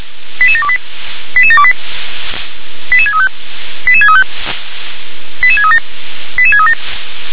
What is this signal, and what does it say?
Five-tone selective call (ZVEI), heard as quick runs of five tones, tones of about 80 ms. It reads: 88728 88728 88635 88635 88637 88637
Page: ZVEI